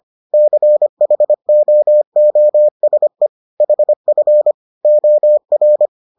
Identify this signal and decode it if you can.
Morse code, heard as a keyed tone, about 25 wpm, 600 Hz. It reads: CHOOSE HF OR